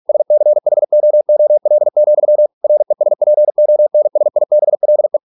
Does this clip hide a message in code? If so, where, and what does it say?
Yes: Morse code from 0.1 to 5.2 s, heard as a keyed tone, about 46 wpm, 600 Hz. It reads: SCHOOL, RESPONSIBLE